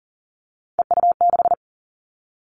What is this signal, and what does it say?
Morse code, heard as a keyed tone, about 38 wpm, 700 Hz. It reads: EU6
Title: Callsign Prefixes for Top 50 DXCC - 40wpm